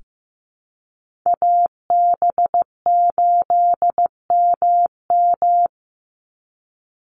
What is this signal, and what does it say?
Morse code, heard as a keyed tone, about 15 wpm, 700 Hz. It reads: AB8MM
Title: USA Call Signs 15wpm